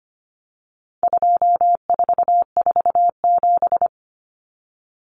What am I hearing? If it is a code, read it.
Morse code, heard as a keyed tone, about 25 wpm, 700 Hz. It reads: 2447
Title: Numbers - 4-Digit 25wpm